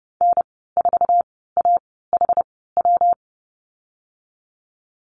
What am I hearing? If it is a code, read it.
Morse code, heard as a keyed tone, about 30 wpm, spaced as for 17 wpm, 700 Hz. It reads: N4AHW
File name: Lesson 19g - Letter w - Call Signs 30wpm